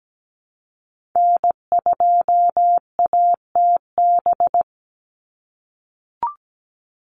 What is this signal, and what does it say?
Morse code, heard as a keyed tone, about 17 wpm, 700 Hz. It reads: N2ATB
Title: Call Signs - 17wpm